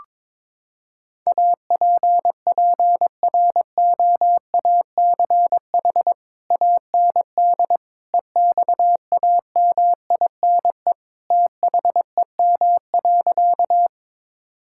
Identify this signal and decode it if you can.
Morse code, heard as a keyed tone, about 22 wpm, 700 Hz. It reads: APPROACH AND EXAMINE THEM.
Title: Sentences from Top 1600 Words 22wpm